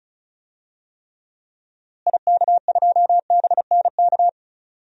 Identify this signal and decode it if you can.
Morse code, heard as a keyed tone, about 36 wpm, 700 Hz. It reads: IK2BNK
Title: DXCC-100 Call Signs - 6 Characters - Repeated 2x - 35wpm